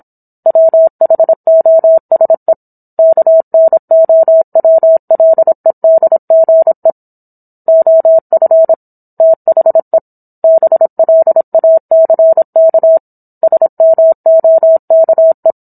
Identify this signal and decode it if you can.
Morse code, heard as a keyed tone, about 27 wpm, 650 Hz. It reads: WHOSE KNOWLEDGE OF THE BLACK SMOKE